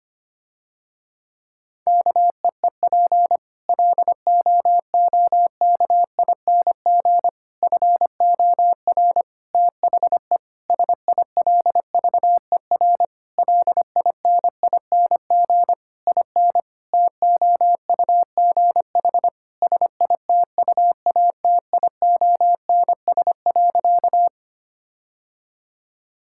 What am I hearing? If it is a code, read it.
Morse code, heard as a keyed tone, about 25 wpm, 700 Hz. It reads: KEEP LOOKING FOR THE SILVER LINING IN TOUGH SITUATIONS.
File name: practice / 25wpm / Difficult Sentences with Hints - 25wpm